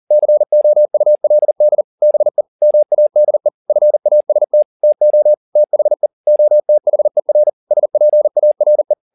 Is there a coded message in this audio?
Yes: Morse code from 0.1 to 8.9 s, heard as a keyed tone, about 40 wpm, 600 Hz. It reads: COULD BE MADE FAST TO THE OTHER SPARE